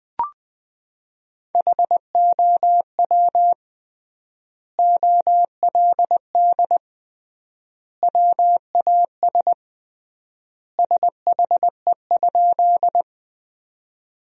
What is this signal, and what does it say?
Morse code, heard as a keyed tone, about 21 wpm, 700 Hz. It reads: HOW OLD WAS SHE?
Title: 3x Word-Spacing - Sentences from Top 200 Words - 20wpm